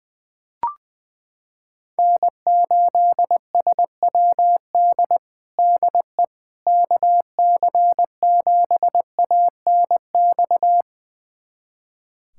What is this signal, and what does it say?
Morse code, heard as a keyed tone, about 20 wpm, 700 Hz. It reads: N8SWD DE KC7ANX
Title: USA Callsign DE Callsign 20wpm